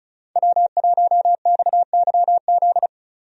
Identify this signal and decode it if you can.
Morse code, heard as a keyed tone, about 36 wpm, 700 Hz. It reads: W1XYZ
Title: USA Call Signs - 5 Characters - 35wpm